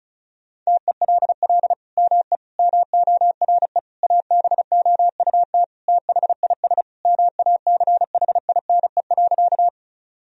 Morse code, heard as a keyed tone, about 34 wpm, 700 Hz. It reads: TELL ME MORE ABOUT THIS MACHINE.
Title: Sentences from Top 1000 Words - Review Part 2 - 35wpm